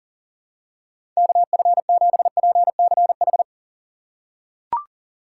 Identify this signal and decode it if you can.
Morse code, heard as a keyed tone, about 40 wpm, 700 Hz. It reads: KF7PCH
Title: USA Call Signs - 4x Character Spacing - 40wpm